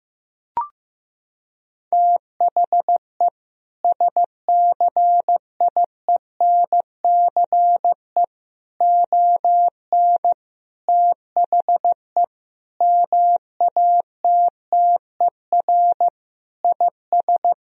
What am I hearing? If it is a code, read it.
Morse code, heard as a keyed tone, about 15 wpm, 700 Hz. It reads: THE SCIENCE ON THE MATTER IS
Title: Sentences from Top 1000 Words - Review Part 2 - 15wpm